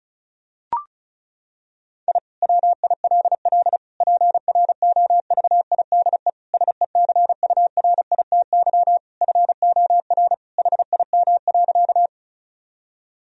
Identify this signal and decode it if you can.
Morse code, heard as a keyed tone, about 36 wpm, 700 Hz. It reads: I WILL PROVIDE SECURITY FOR HIM.
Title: Sentences from Top 1000 Words - Review Part 4 - 35wpm